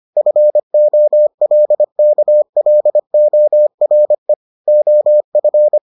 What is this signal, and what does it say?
Morse code, heard as a keyed tone, about 25 wpm, 600 Hz. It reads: FOLKLORE OF